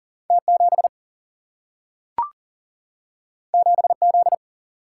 Morse code, heard as a keyed tone, about 38 wpm, 700 Hz. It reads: T 7 7 Z
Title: Single-Letter Single-Number 40wpm